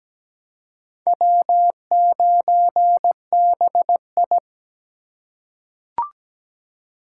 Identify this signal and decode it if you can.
Morse code, heard as a keyed tone, about 17 wpm, 700 Hz. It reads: W9BI
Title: USA Call Signs - 4 Characters - 3x Character Spacing - 17wpm